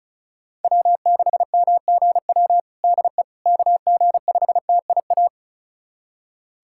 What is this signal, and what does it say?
Morse code, heard as a keyed tone, about 34 wpm, 700 Hz. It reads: W6MGW DE KG5TIA
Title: USA Callsign DE Callsign 35wpm